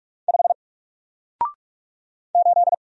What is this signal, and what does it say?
Morse code, heard as a keyed tone, about 44 wpm, 700 Hz. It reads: 5 8
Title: Single Character (Letter:Number:Punctuation:Prosign) Alt 45wpm